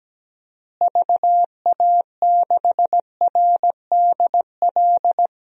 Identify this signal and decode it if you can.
Morse code, heard as a keyed tone, about 17 wpm, 700 Hz. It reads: VA6RDL
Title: DXCC-100 Call Signs - 6 Characters - Repeated 3x - 17wpm